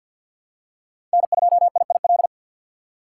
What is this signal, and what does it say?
Morse code, heard as a keyed tone, about 53 wpm, 700 Hz. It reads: N2IIL
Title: USA Call Signs - 5 Characters - 2x Character Spacing - 50wpm